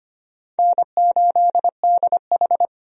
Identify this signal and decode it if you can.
Morse code, heard as a keyed tone, about 25 wpm, 700 Hz. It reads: N8DH
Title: USA Call Signs - 4 Characters - 25wpm